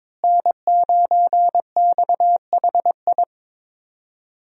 Morse code, heard as a keyed tone, about 22 wpm, 700 Hz. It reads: N9XHI
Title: USA Call Signs - 5 Characters - Repeated 3x - 22wpm